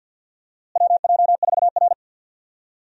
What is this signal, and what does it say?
Morse code, heard as a keyed tone, about 49 wpm, 700 Hz. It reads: WJ4R
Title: USA Call Signs - 4 Characters - Repeated 3x - 50wpm